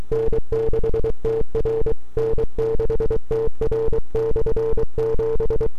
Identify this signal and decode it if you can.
Morse code, heard as a keyed tone, about 24 wpm, 450 Hz. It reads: N6TR N6TR/7
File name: n6tr / 12Oct2009-160